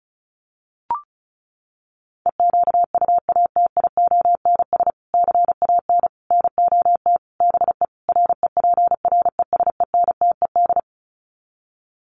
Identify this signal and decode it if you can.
Morse code, heard as a keyed tone, about 34 wpm, 700 Hz. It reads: EQUATIONS CAN NOT BE REPRESENTED
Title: Sets of 5 Words 35wpm